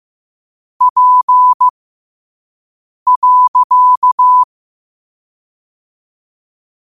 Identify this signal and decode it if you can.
Morse code, heard as a keyed tone, about 15 wpm, spaced as for 5 wpm, 1000 Hz. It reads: P.
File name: qotdfiles / 5wpm